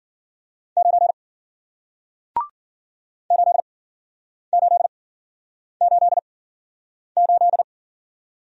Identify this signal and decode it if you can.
Morse code, heard as a keyed tone, about 53 wpm, spaced as for 9 wpm, 700 Hz. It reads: 9 8888